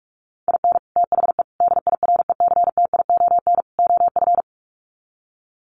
Morse code, heard as a keyed tone, about 45 wpm, 700 Hz. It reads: IN THE DIRECTION OF
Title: Sets of 4 Words 45wpm